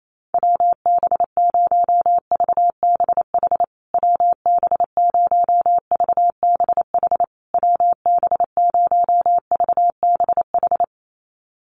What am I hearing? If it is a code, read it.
Morse code, heard as a keyed tone, about 28 wpm, 700 Hz. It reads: WB0VBH WB0VBH WB0VBH
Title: USA Call Signs Repeated 3x 28wpm